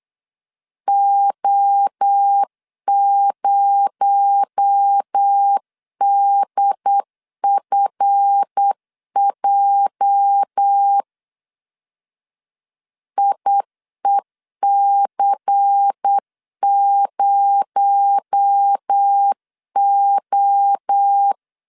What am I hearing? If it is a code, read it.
Morse code, heard as a keyed tone, about 8 wpm, 800 Hz. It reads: O0DFJ IEC0O
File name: LEZIONI / 8 YJ45 / 8 YJ45 8wpm 3214